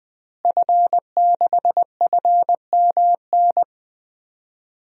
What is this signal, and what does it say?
Morse code, heard as a keyed tone, about 20 wpm, 700 Hz. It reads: F6FMN
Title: DXCC-100 Call Signs - 5 Characters - Repeated 3x - 20wpm